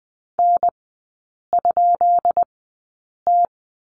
Morse code, heard as a keyed tone, about 20 wpm, spaced as for 8 wpm, 700 Hz. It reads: N?T
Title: 133 Lesson 22e - Question Mark - Three Characters 20wpm